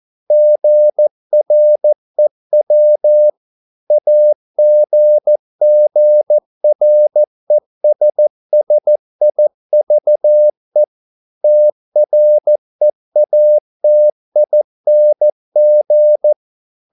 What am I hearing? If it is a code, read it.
Morse code, heard as a keyed tone, about 14 wpm, 600 Hz. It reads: GREW AGGRESSIVE TREATING
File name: mp3 files / a dogs mistake - 14wpm